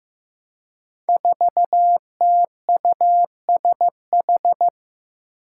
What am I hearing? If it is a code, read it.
Morse code, heard as a keyed tone, about 16 wpm, 700 Hz. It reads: 4TUSH
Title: Five Challenging Letter-Number Combinations 15wpm